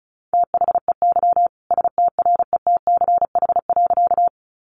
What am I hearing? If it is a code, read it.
Morse code, heard as a keyed tone, about 34 wpm, 700 Hz. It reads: THEY STRETCH.
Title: Easy Phrases 35wpm